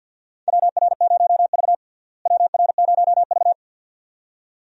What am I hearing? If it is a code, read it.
Morse code, heard as a keyed tone, about 51 wpm, 700 Hz. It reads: WR0V WR0V
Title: USA Call Signs - 4 Characters - Repeated 2x - 50wpm